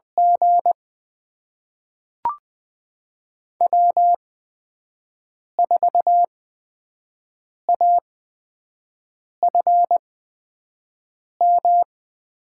Morse code, heard as a keyed tone, about 20 wpm, spaced as for 5 wpm, 700 Hz. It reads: G W4AFM
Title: USA Call Signs - 5 Characters - 8x Character Spacing - 20wpm